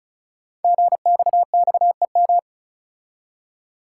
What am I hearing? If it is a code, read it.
Morse code, heard as a keyed tone, about 36 wpm, 690 Hz. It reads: GXXEM
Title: Code Groups - 5 Letters - 35wpm